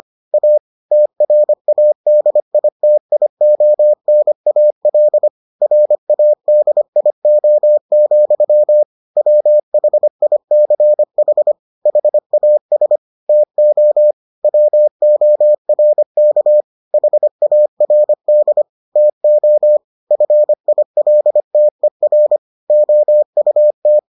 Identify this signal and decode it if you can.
Morse code, heard as a keyed tone, about 25 wpm, 600 Hz. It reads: A TRADITIONAL RADIO, WHICH HAS TO WORK HARD TO FILTER OUT